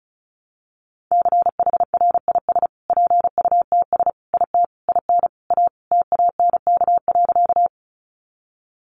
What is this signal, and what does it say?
Morse code, heard as a keyed tone, about 34 wpm, 700 Hz. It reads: CHRIS PUTS IT IN A TANK.
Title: Easy Phrases 35wpm